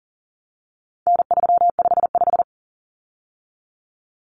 Morse code, heard as a keyed tone, about 40 wpm, 700 Hz. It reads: N355
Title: Common Cut Numbers - 4-Digit 40wpm